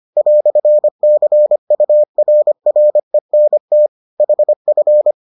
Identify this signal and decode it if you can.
Morse code, heard as a keyed tone, about 25 wpm, 600 Hz. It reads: "CURRENT HF